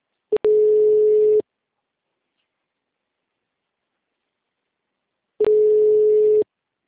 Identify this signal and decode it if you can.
Ringback tone (Europe, 425 Hz).